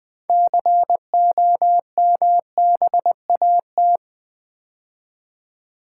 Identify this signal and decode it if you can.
Morse code, heard as a keyed tone, about 20 wpm, 700 Hz. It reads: COMBAT
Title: 6 Letter Words - First 3 Letters Given - 20wpm